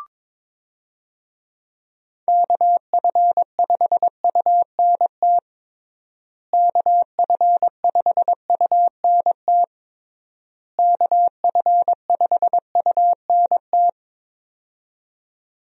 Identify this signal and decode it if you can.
Morse code, heard as a keyed tone, about 22 wpm, 700 Hz. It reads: KF5UNT KF5UNT KF5UNT
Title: USA Call Signs - 6 Characters - Repeated 3x - 22wpm